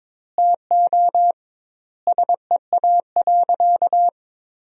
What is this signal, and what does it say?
Morse code, heard as a keyed tone, about 22 wpm, 700 Hz. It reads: TO SEA.
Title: 2x Word-Spacing - Sentences from Top 400 Words - 22wpm